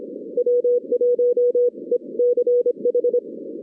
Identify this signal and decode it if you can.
Morse code, heard as a keyed tone, about 24 wpm, 470 Hz. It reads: W1ECH
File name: 2016 arrl dx cw / 2016-ARRL-DX-CW-K5TR- - 0-22